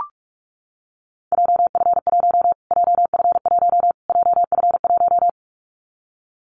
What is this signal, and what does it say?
Morse code, heard as a keyed tone, about 44 wpm, 700 Hz. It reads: JF1 JF1 JF1
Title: Callsign Prefixes for Top 25 DXCC - Repeated 3x - 45wpm